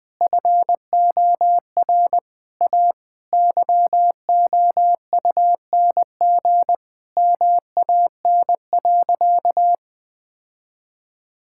Morse code, heard as a keyed tone, about 20 wpm, 700 Hz. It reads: FOR A YOUNG MAN.
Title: Sentences from Top 1000 Words - Review Part 5 - 20wpm